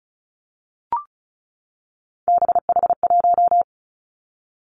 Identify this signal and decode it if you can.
Morse code, heard as a keyed tone, about 36 wpm, 700 Hz. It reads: BH1
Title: Three Challenging Letter-Number Combinations 35wpm